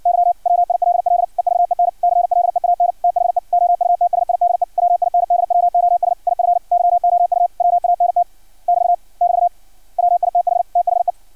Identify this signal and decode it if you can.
Morse code, heard as a keyed tone, about 56 wpm, 700 Hz. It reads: Y WILL EVEN CLEAN THE OUTSIDE WINDWOS IF YOU WANT <BT> <BT> WITH THE
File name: archived / CRSnet